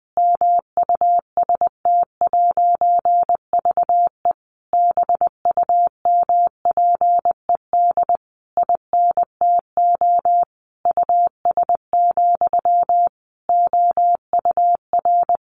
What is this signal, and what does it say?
Morse code, heard as a keyed tone, about 20 wpm, 700 Hz. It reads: MUST'VE BUMPED INTO US, OUR
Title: A Sound Of Thunder 20wpm